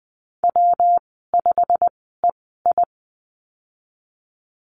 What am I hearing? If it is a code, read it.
Morse code, heard as a keyed tone, about 20 wpm, spaced as for 15 wpm, 700 Hz. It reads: W5EI
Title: 147 Lesson 24g - Letter y - Call Signs 20wpm